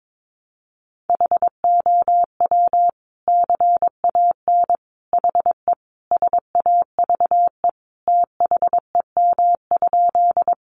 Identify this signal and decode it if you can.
Morse code, heard as a keyed tone, about 22 wpm, 700 Hz. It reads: HOW CAN HE SAVE THEM?
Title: Sentences from Top 1100 Words 22wpm